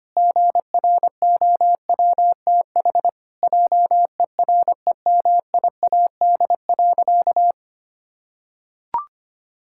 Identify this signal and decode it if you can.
Morse code, heard as a keyed tone, about 25 wpm, 700 Hz. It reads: GROWTH JEREMIAD.
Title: Pangram Challenge v2 - 25wpm